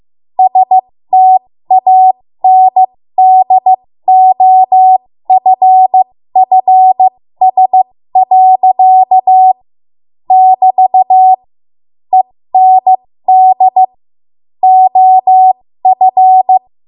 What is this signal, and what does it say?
Morse code, heard as a keyed tone, about 15 wpm, spaced as for 13 wpm, 750 Hz. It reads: STANDOFFS. <BT> END OF
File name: Morse / 13WPM